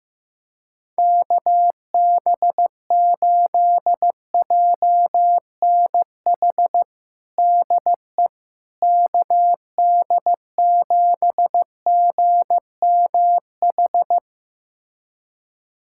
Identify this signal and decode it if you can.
Morse code, heard as a keyed tone, about 15 wpm, 700 Hz. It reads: KB8JNH DE KD7GMH
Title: USA Callsign DE Callsign - 6 Characters - 15wpm